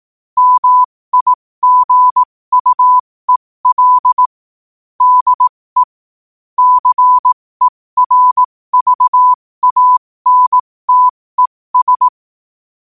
Morse code, heard as a keyed tone, about 18 wpm, spaced as for 15 wpm, 1000 Hz. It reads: MIGUEL DE CERVANTES
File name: qotdfiles / 15wpm